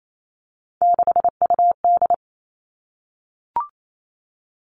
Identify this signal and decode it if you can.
Morse code, heard as a keyed tone, about 28 wpm, 700 Hz. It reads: 6UD
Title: All Cut Numbers - 3-Digit 28wpm